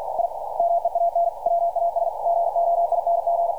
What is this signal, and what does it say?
Morse code, heard as a keyed tone, about 50 wpm, 690 Hz. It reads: I MEAN WAS GOING
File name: archived / CRSnet